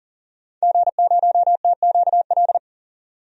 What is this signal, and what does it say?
Morse code, heard as a keyed tone, about 40 wpm, 700 Hz. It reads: G0TQL
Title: DXCC-100 Call Signs - 5 Characters - 40wpm